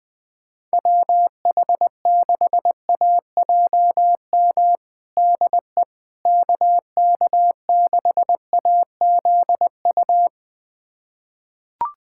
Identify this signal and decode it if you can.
Morse code, heard as a keyed tone, about 20 wpm, 700 Hz. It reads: WH6AJM DE KK6AZU
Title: USA Callsign DE Callsign 20wpm